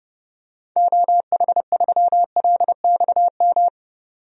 Morse code, heard as a keyed tone, about 30 wpm, 700 Hz. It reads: OH3LXM
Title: DXCC-100 Call Signs - 6 Characters - 30wpm